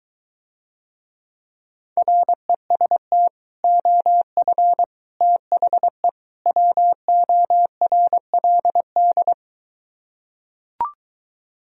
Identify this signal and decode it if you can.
Morse code, heard as a keyed tone, about 23 wpm, 700 Hz. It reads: REST OF THE WORLD